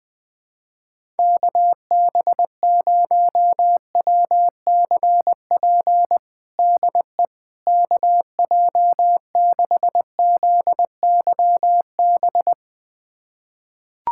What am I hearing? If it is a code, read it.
Morse code, heard as a keyed tone, about 20 wpm, 700 Hz. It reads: KB0WCP DE KJ6ZYB
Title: USA Callsign DE Callsign - 6 Characters - 20wpm